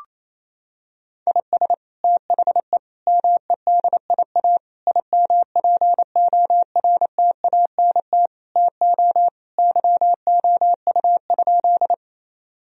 Morse code, heard as a keyed tone, about 28 wpm, 700 Hz. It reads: IS THE MEDIA IMPORTANT TO YOU?
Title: Sentences from Top 1000 Words - Review Part 4 - 28wpm